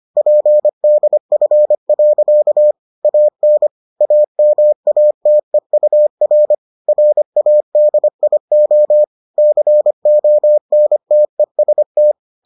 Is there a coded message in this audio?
Yes: Morse code from 0.2 to 12.1 s, heard as a keyed tone, about 25 wpm, 600 Hz. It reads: PDF. AN AMATEUR RADIO CONTEST